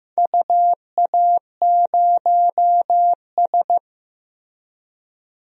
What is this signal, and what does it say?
Morse code, heard as a keyed tone, about 15 wpm, 700 Hz. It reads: UA0S
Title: DXCC-100 Call Signs - 4 Characters - Repeated 3x - 15wpm